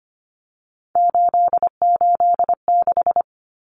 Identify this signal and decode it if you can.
Morse code, heard as a keyed tone, about 25 wpm, 700 Hz. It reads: 886
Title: Numbers - 3-Digit 25wpm